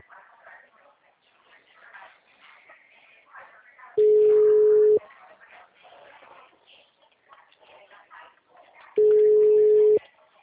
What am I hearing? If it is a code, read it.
Ringback tone (Europe, 425 Hz).
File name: out- -2004